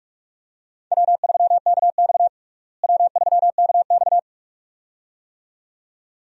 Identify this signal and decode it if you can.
Morse code, heard as a keyed tone, about 47 wpm, 700 Hz. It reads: W3KX W3KX
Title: USA Call Signs - 4 Characters - Repeated 2x - 45wpm